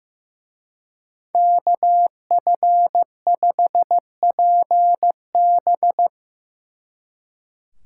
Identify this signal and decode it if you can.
Morse code, heard as a keyed tone, about 18 wpm, 700 Hz. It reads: KF5PB
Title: Speed Racing - USA 2x2 Call Signs 15wpm